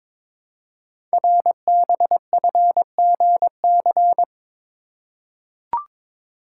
Morse code, heard as a keyed tone, about 22 wpm, 700 Hz. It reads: RBFGC
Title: Code Groups - 5 Letters - 22wpm